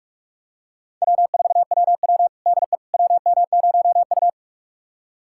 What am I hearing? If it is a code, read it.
Morse code, heard as a keyed tone, about 43 wpm, 700 Hz. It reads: W4WW DE WM0U
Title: USA Callsign DE Callsign - 4 Characters - 45wpm